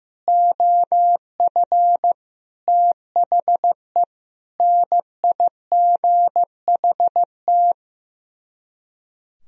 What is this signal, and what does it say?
Morse code, heard as a keyed tone, about 15 wpm, 700 Hz. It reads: OF THE NIGHT
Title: Speed-Racing - Sets of 5 Words 15wpm